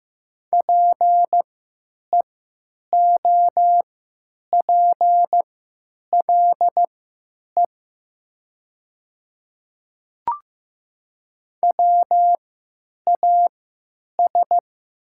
Morse code, heard as a keyed tone, about 15 wpm, spaced as for 9 wpm, 700 Hz. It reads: PEOPLE WAS
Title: 3x Character Spacing - Top 200 Words - 15wpm